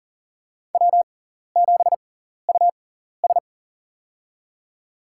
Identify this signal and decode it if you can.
Morse code, heard as a keyed tone, about 40 wpm, spaced as for 14 wpm, 700 Hz. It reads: W7US
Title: USA Call Signs - 4 Characters - 6x Character Spacing - 40wpm